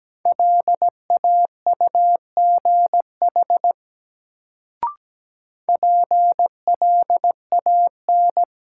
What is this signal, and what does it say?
Morse code, heard as a keyed tone, about 17 wpm, 700 Hz. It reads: LAUGH PLAN
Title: Sentences from Top 500 Words - New Word Drill - 17wpm